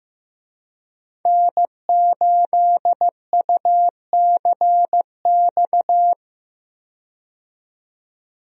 Morse code, heard as a keyed tone, about 15 wpm, 700 Hz. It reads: N8UCX
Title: USA Call Signs - 5 Characters - 15wpm